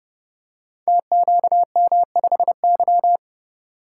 Morse code, heard as a keyed tone, about 30 wpm, 700 Hz. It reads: TQM5Y
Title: Code Groups - 5 Letter:Numbers - 30wpm